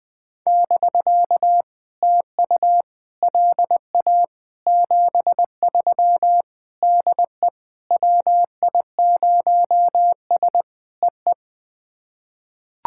Morse code, heard as a keyed tone, about 20 wpm, 700 Hz. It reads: <BK> TU LA 73 DE WI0S EE
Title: POTA QSO Practice - 20wpm